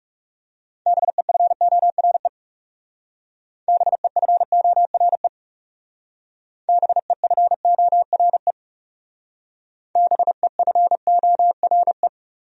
Morse code, heard as a keyed tone, about 36 wpm, 700 Hz. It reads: BEFORE BEFORE BEFORE BEFORE
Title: Speed-Racing Top 300 Words 30wpm